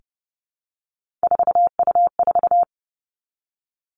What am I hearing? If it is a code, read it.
Morse code, heard as a keyed tone, about 32 wpm, 700 Hz. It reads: 4U4
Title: Three Challenging Letter-Number Combinations 30wpm